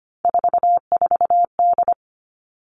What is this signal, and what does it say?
Morse code, heard as a keyed tone, about 25 wpm, 700 Hz. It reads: 44D
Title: All Cut Numbers - 3-Digit 25wpm